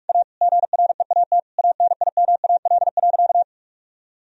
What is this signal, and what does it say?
Morse code, heard as a keyed tone, about 46 wpm, 700 Hz. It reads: A GREAT ANIMAL.
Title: Sentences from Top 1000 Words - Review Part 1 - 45wpm